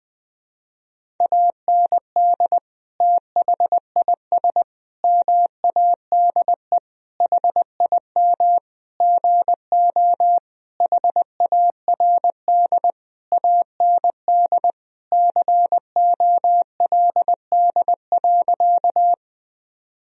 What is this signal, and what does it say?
Morse code, heard as a keyed tone, about 20 wpm, 700 Hz. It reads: AND THIS MADE HIM GO HARD AND COLD.
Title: Sentences from Top 1000 Words - Review Part 1 - 20wpm